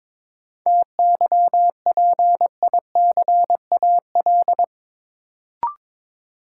Morse code, heard as a keyed tone, about 22 wpm, 700 Hz. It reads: TYPICAL
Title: Sentences from Top 1100 Words - New Word Drill Alt - 22wpm